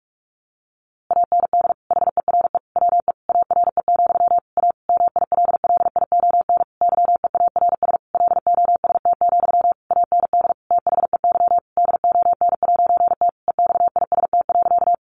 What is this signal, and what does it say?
Morse code, heard as a keyed tone, about 44 wpm, 700 Hz. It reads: AND HERE WE ARE, A MILLION YEARS LOST, AND THEY DON'T EXIST.
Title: A Sound Of Thunder 45wpm